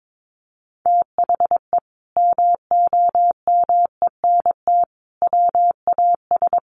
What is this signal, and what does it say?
Morse code, heard as a keyed tone, about 22 wpm, 700 Hz. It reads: THE MOMENT WAS
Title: Sentences from Top 1100 Words 22wpm